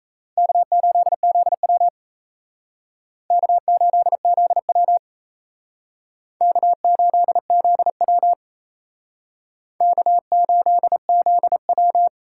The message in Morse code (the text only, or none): K8ZW K8ZW K8ZW K8ZW